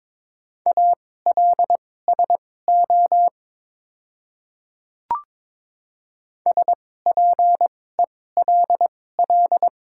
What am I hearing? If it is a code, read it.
Morse code, heard as a keyed tone, about 22 wpm, spaced as for 16 wpm, 700 Hz. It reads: ALSO SPELL
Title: 2x Character Spacing - Top 300 Words - 22wpm